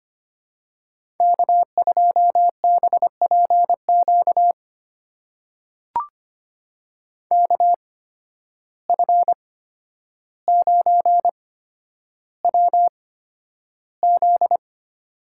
Morse code, heard as a keyed tone, about 25 wpm, spaced as for 11 wpm, 700 Hz. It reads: K2BPQ KF9WZ
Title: USA Call Signs - 5 Characters - 8x Character Spacing - 25wpm